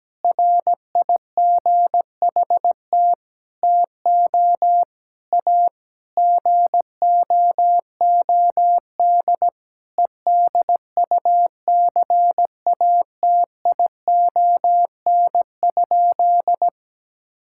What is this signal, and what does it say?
Morse code, heard as a keyed tone, about 17 wpm, 700 Hz. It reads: RIGHT TO A GOOD EDUCATION?
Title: Sentences from Top 1000 Words - Review Part 6 - 17wpm